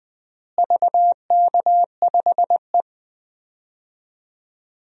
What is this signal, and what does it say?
Morse code, heard as a keyed tone, about 20 wpm, 700 Hz. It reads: VK5E
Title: DXCC-100 Call Signs - 4 Characters - Repeated 2x - 20wpm